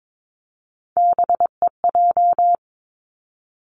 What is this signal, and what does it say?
Morse code, heard as a keyed tone, about 22 wpm, 700 Hz. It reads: BEJ
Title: Three Challenging Letter-Number Combinations 22wpm